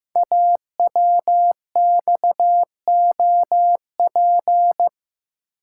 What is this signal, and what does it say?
Morse code, heard as a keyed tone, about 15 wpm, 700 Hz. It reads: AWXOP
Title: Code Groups - 5 Letters - 15wpm